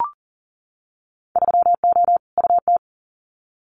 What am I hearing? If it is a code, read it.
Morse code, heard as a keyed tone, about 37 wpm, 700 Hz. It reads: 3O UT
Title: Winter Field Day Exchanges 40wpm